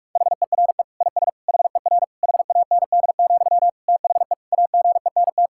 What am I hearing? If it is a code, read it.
Morse code, heard as a keyed tone, about 46 wpm, 700 Hz. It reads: HERE IS HER HAND, THE AGENT
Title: Sentences from Top 1600 Words 45wpm